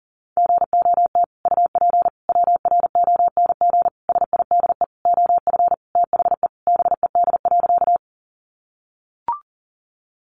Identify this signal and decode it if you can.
Morse code, heard as a keyed tone, about 40 wpm, 700 Hz. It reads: GOT UP WRONG SIDE OF THE BED.
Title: Sentences from Top 900 Words 40wpm